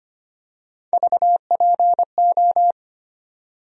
Morse code, heard as a keyed tone, about 25 wpm, 700 Hz. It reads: VPO
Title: USA Call Sign Postfixes - 3 Letters 25wpm